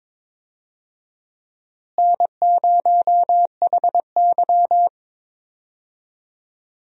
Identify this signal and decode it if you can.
Morse code, heard as a keyed tone, about 22 wpm, 700 Hz. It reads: N0HY
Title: USA Call Signs - 4 Characters - Repeated 3x - 22wpm